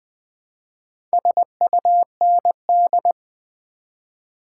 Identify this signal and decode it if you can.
Morse code, heard as a keyed tone, about 21 wpm, 700 Hz. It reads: SUND
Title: Build A Word - 2 Letters At A Time - 6-Letter Words - 20wpm